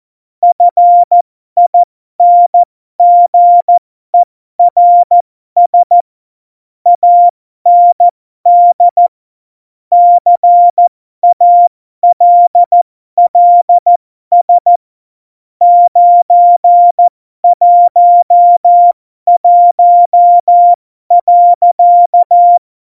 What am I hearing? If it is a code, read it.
Morse code, heard as a keyed tone, about 14 wpm, spaced as for 12 wpm, 700 Hz. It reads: FINGERS AND CALLS 911.
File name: Morsekurs / Witze